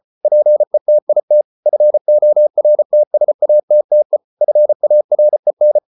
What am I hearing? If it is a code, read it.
Morse code, heard as a keyed tone, about 34 wpm, 600 Hz. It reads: PETIT FORTSATTE FAREN
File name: Ferie QRQ 34wpm